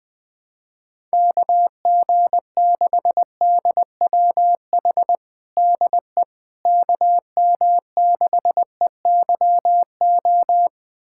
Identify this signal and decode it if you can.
Morse code, heard as a keyed tone, about 20 wpm, 700 Hz. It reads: KG6DWH DE KM6EYO
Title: USA Callsign DE Callsign - 6 Characters - 20wpm